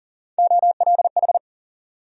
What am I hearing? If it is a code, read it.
Morse code, heard as a keyed tone, about 40 wpm, 700 Hz. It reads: OLH
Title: USA Call Sign Postfixes - 3 Letters 40wpm